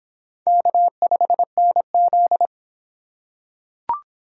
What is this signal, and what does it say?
Morse code, heard as a keyed tone, about 23 wpm, 700 Hz. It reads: K5NZ